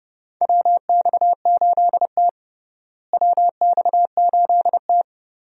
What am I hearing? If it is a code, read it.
Morse code, heard as a keyed tone, about 31 wpm, 700 Hz. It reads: WX8T WX8T
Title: USA Call Signs - 4 Characters - Repeated 3x - 30wpm